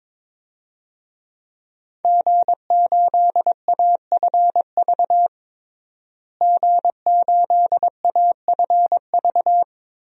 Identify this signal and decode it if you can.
Morse code, heard as a keyed tone, about 22 wpm, 700 Hz. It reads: G8AFV G8AFV
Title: DXCC-100 Call Signs - 5 Characters - Repeated 2x - 22wpm